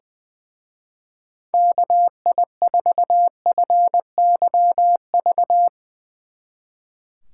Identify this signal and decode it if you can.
Morse code, heard as a keyed tone, about 22 wpm, 700 Hz. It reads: KI4FYV
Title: Speed Racing - USA Call Signs 20wpm